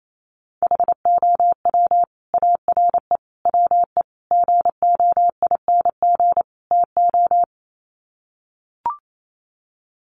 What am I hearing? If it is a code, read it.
Morse code, heard as a keyed tone, about 28 wpm, 700 Hz. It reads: HOW ARE WE GOING TO
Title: Sets of 5 Words Encore 28wpm